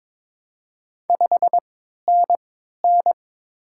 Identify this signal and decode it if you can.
Morse code, heard as a keyed tone, about 22 wpm, 700 Hz. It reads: 5 N N
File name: Top 20 QSO Elements - 3x Character Spacing - 22wpm